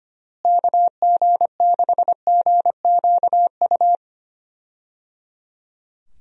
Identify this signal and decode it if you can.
Morse code, heard as a keyed tone, about 25 wpm, 700 Hz. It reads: KG6GQU
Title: USA Call Signs - 6 Characters - Repeated 3x - 25wpm